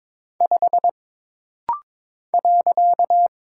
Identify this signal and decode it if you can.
Morse code, heard as a keyed tone, about 22 wpm, 700 Hz. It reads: 5 .
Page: This rapid-fire timing is geared to push your ICR (Instant Character Recognition) skills.